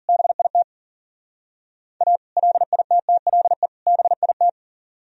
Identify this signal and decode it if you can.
Morse code, heard as a keyed tone, about 50 wpm, 700 Hz. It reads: BIT A LITTLE BIT